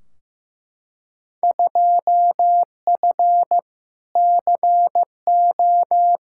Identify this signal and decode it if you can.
Morse code, heard as a keyed tone, about 15 wpm, 700 Hz. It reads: 2F CO
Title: Field Day Exchanges - 15wpm